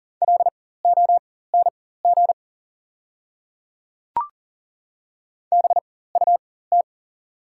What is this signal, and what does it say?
Morse code, heard as a keyed tone, about 37 wpm, spaced as for 18 wpm, 690 Hz. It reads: LONG BUT
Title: 4x Character Spacing - Top 500 Words - 40wpm